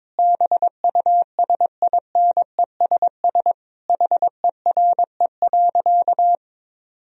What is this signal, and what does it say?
Morse code, heard as a keyed tone, about 22 wpm, 700 Hz. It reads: BUSINESS HERE.
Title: Sentences from Top 1000 Words - Review Part 4 - 22wpm